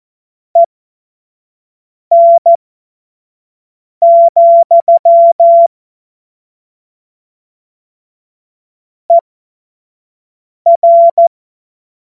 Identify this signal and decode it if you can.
Morse code, heard as a keyed tone, about 14 wpm, spaced as for 5 wpm, 680 Hz. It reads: EN, ER
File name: Morsekurs / Witze / Witz 14. Februar 2023 25 F70